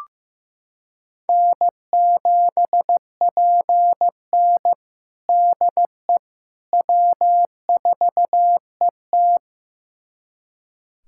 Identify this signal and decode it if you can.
Morse code, heard as a keyed tone, about 15 wpm, 700 Hz. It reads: N7PN DE W4ET
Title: USA Callsign DE Callsign - 4 Characters - 15wpm